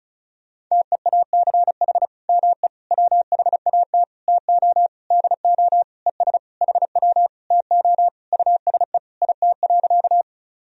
Morse code, heard as a keyed tone, about 36 wpm, 700 Hz. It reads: TEACH ME WHAT TO DO ES HW TO USE IT.
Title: Sentences from Top 500 Words and Abbreviations 35wpm